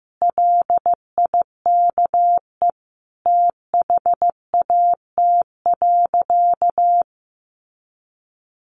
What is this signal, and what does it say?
Morse code, heard as a keyed tone, about 15 wpm, 700 Hz. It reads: LIKE THAT.
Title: Sentences from Top 500 Words 15wpm